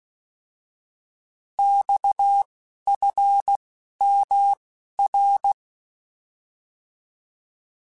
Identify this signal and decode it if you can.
Morse code, heard as a keyed tone, about 16 wpm, 800 Hz. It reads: X F M R
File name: Abreviations / Abreviations mp3s / Abreviations@10